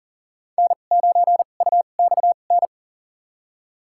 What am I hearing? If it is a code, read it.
Morse code, heard as a keyed tone, about 39 wpm, spaced as for 29 wpm, 700 Hz. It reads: N9UXN
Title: USA Call Signs - 5 Characters - 2x Character Spacing - 40wpm